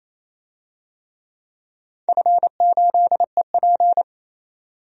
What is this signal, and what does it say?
Morse code, heard as a keyed tone, about 27 wpm, 700 Hz. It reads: F8EP
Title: DXCC-100 Call Signs - 4 Characters - 28wpm